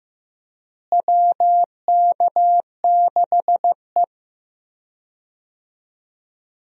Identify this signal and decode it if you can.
Morse code, heard as a keyed tone, about 15 wpm, 700 Hz. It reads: WK6E
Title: USA Call Signs - 4 Characters - Repeated 3x - 15wpm